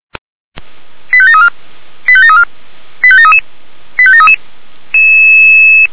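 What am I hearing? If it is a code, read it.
Five-tone selective call (ZVEI), heard as quick runs of five tones, tones of about 70 ms. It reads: 85734 85734 85733 85733
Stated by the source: ZVEI five-tone selective call received on an SDR.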